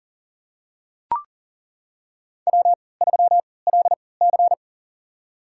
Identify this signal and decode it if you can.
Morse code, heard as a keyed tone, about 40 wpm, spaced as for 23 wpm, 700 Hz. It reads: W3LC
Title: USA Call Signs - 4 Characters - 3x Character Spacing - 40wpm